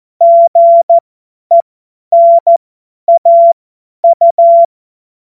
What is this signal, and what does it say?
Morse code, heard as a keyed tone, about 14 wpm, spaced as for 10 wpm, 680 Hz. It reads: GENAU
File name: Witz-14-Oktober-2025-50-F70